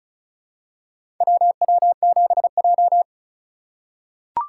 Morse code, heard as a keyed tone, about 34 wpm, 700 Hz. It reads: WW7J
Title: USA Call Signs - 4 Characters - 4x Character Spacing - 35wpm